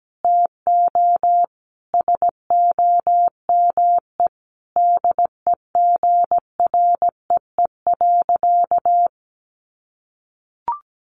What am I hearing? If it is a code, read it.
Morse code, heard as a keyed tone, about 17 wpm, 700 Hz. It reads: TO SOME DEGREE.
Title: Sentences from Top 1500 Words 17wpm